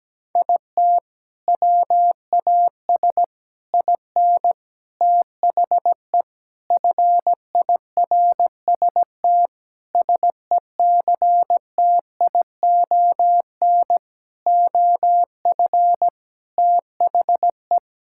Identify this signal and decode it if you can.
Morse code, heard as a keyed tone, about 17 wpm, 700 Hz. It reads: IT WAS IN THE FIRST SECTION OF THE
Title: Sentences from Top 1000 Words - Review Part 5 - 17wpm